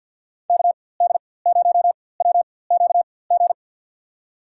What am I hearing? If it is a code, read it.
Morse code, heard as a keyed tone, about 51 wpm, spaced as for 23 wpm, 700 Hz. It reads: KD0WQG
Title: USA Call Signs - 6 Characters - 4x Character Spacing - 50wpm